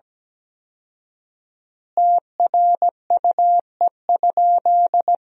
Morse code, heard as a keyed tone, about 17 wpm, 700 Hz. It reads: TRUE?
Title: 4x Word-Spacing - Sentences from Top 400 Words - 17wpm